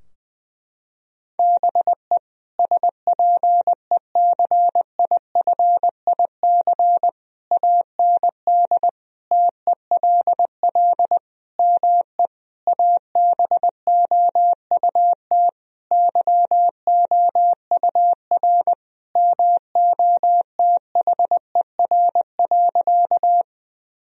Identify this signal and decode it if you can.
Morse code, heard as a keyed tone, about 20 wpm, 700 Hz. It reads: BE SPECIFIC AND TELL ME ABOUT YOUR MOTHER.
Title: Sentences from Top 1000 Words - Review Part 3 - 20wpm